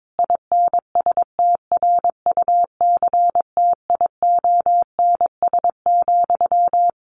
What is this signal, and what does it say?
Morse code, heard as a keyed tone, about 22 wpm, 700 Hz. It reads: INSTRUCTIONS,
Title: A Sound Of Thunder 22wpm